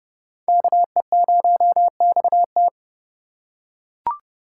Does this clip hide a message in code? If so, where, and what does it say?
Yes: Morse code from 0.5 to 2.7 s, heard as a keyed tone, about 29 wpm, 700 Hz. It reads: KE0XT